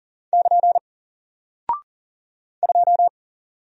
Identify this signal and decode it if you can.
Morse code, heard as a keyed tone, about 40 wpm, 700 Hz. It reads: <KN> 2
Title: Single Character (Letter:Number:Punctuation:Prosign) 40wpm